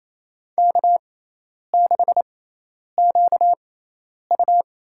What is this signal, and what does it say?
Morse code, heard as a keyed tone, about 28 wpm, spaced as for 10 wpm, 700 Hz. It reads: K6QU
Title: USA Call Signs - 4 Characters - 6x Character Spacing - 28wpm